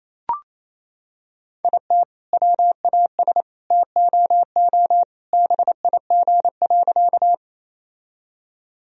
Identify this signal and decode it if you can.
Morse code, heard as a keyed tone, about 28 wpm, 700 Hz. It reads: IT WAS TOO BIG.
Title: Sentences from Top 1000 Words - Review Part 1 - 28wpm